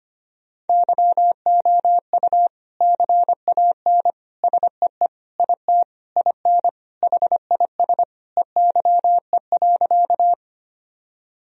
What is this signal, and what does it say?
Morse code, heard as a keyed tone, about 25 wpm, 700 Hz. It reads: YOU CAN SEE IT IN HIS EYE.
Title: Sentences from Top 1000 Words - Review Part 1 - 25wpm